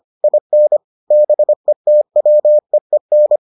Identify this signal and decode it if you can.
Morse code, heard as a keyed tone, about 25 wpm, 600 Hz. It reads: IN BETWEEN